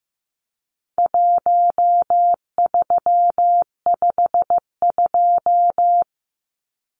Morse code, heard as a keyed tone, about 15 wpm, 700 Hz. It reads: 1352
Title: Numbers - Contest Style - 4-Digit 15wpm